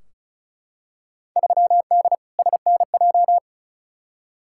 Morse code, heard as a keyed tone, about 36 wpm, 700 Hz. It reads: 3D SNJ
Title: Field Day Exchanges Encore - 35wpm